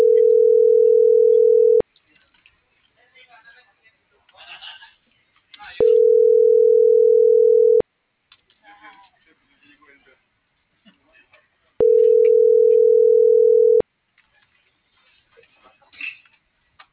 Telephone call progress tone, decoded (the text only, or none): ringback tone